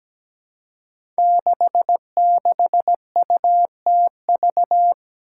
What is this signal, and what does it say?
Morse code, heard as a keyed tone, about 17 wpm, 700 Hz. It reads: 66UTV
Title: Five Challenging Letter-Number Combinations 17wpm